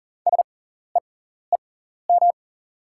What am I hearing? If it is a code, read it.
Morse code, heard as a keyed tone, about 37 wpm, spaced as for 14 wpm, 690 Hz. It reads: SEEM
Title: 6x Character Spacing - Top 300 Words - 40wpm